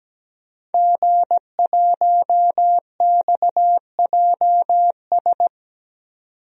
Morse code, heard as a keyed tone, about 17 wpm, 700 Hz. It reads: G1XJS
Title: DXCC-100 Call Signs - 5 Characters - 17wpm